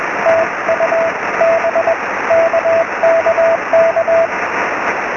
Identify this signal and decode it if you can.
Morse code, heard as a keyed tone, about 20 wpm, 660 Hz, with radio noise. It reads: TU B KKK
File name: sm7lcb / 11-oz2ld-23cm